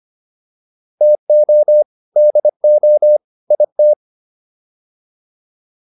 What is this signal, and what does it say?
Morse code, heard as a keyed tone, about 25 wpm, 600 Hz. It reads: TO DO IT